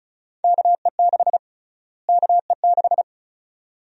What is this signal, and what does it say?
Morse code, heard as a keyed tone, about 36 wpm, 700 Hz. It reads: KE6 KE6
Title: USA Call Sign Prefixes - Repeated 2x - 35wpm